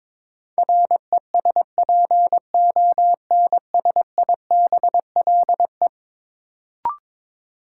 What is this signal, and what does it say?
Morse code, heard as a keyed tone, about 22 wpm, 700 Hz. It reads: RESPONSIBLE
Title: Sentences from Top 1600 Words - New Word Drill Alt - 22wpm